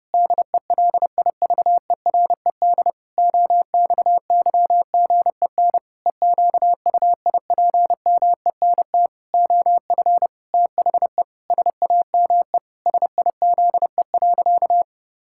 Morse code, heard as a keyed tone, about 30 wpm, 700 Hz. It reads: DELIVERED OXYGEN EQUIPMENT OF THE SAME SIZE.
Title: Pangram Challenge v2 - 30wpm